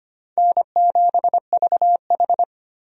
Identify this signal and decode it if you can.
Morse code, heard as a keyed tone, about 25 wpm, 700 Hz. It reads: N7VH
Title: USA Call Signs - 4 Characters - 25wpm